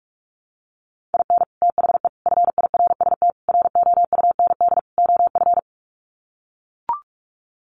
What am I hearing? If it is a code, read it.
Morse code, heard as a keyed tone, about 44 wpm, 700 Hz. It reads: IN THE FIRST ROUND OF
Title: Sets of 5 Words Encore 45wpm